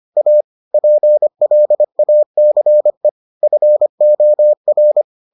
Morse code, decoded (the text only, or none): A PLACE FOR